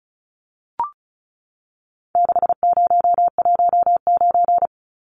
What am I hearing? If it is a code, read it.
Morse code, heard as a keyed tone, about 34 wpm, 700 Hz. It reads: 6019
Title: Numbers - 4-Digit 35wpm